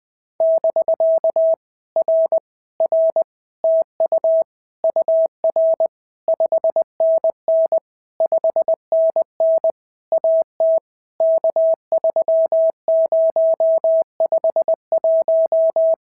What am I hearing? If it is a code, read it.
Morse code, heard as a keyed tone, about 20 wpm, 650 Hz. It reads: <BK> R R TU UR 5NN 5NN AT K3051
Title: POTA QSO Practice - 20wpm